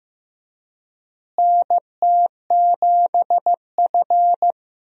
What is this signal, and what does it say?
Morse code, heard as a keyed tone, about 15 wpm, 700 Hz. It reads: NT7F
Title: USA Call Signs - 4 Characters - Repeated 3x - 15wpm